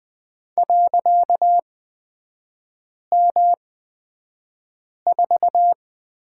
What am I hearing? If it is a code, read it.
Morse code, heard as a keyed tone, about 20 wpm, spaced as for 5 wpm, 700 Hz. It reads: .M4
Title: 255 Lesson 41e - Period - Three Characters 20wpm